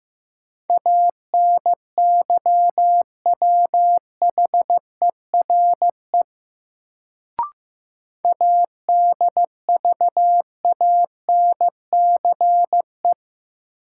Morse code, heard as a keyed tone, about 15 wpm, 700 Hz. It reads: ANYWHERE ADVANCE
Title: Sentences from Top 1400 Words - New Word Drill - 15wpm